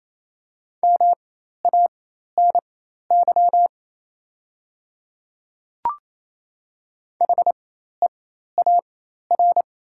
Morse code, heard as a keyed tone, about 28 wpm, spaced as for 13 wpm, 700 Hz. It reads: MANY HEAR
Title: 4x Character Spacing - Top 300 Words - 28wpm